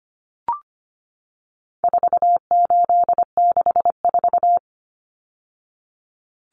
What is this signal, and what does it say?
Morse code, heard as a keyed tone, about 25 wpm, 700 Hz. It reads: 4864
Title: Numbers - 4-Digit 25wpm